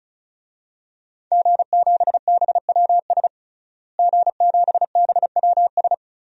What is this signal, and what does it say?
Morse code, heard as a keyed tone, about 34 wpm, 700 Hz. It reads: G7BWS G7BWS
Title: DXCC-100 Call Signs - 5 Characters - Repeated 3x - 35wpm